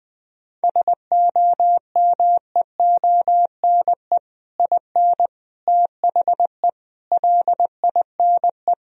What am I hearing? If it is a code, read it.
Morse code, heard as a keyed tone, about 21 wpm, 700 Hz. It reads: SOMEONE IN THE LINE
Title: Sentences from Top 1000 Words - Review Part 1 - 20wpm